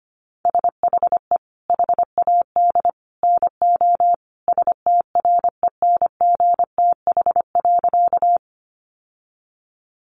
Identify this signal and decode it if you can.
Morse code, heard as a keyed tone, about 25 wpm, 700 Hz. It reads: SHE HAD NO STRENGTH.
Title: Sentences from Top 1300 Words 25wpm